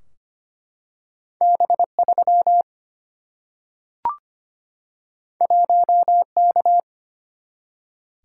Morse code, heard as a keyed tone, about 25 wpm, 700 Hz. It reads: B3 1K
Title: Single-Letter Single-Number 25wpm